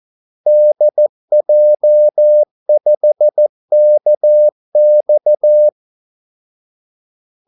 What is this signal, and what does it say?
Morse code, heard as a keyed tone, about 14 wpm, 600 Hz. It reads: DJ5KX
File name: Callsigns-14wpm